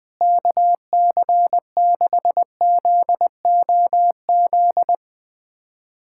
Morse code, heard as a keyed tone, about 20 wpm, 700 Hz. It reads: KC6ZOZ
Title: USA Callsign DE Callsign - 6 Characters - 20wpm